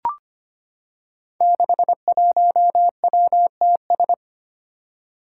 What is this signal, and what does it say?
Morse code, heard as a keyed tone, about 25 wpm, 700 Hz. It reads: 61WTS
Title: Five Challenging Letter-Number Combinations 25wpm